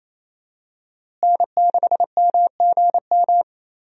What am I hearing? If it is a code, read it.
Morse code, heard as a keyed tone, about 28 wpm, 700 Hz. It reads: N6MGM
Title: USA Call Signs - 5 Characters - Repeated 3x - 28wpm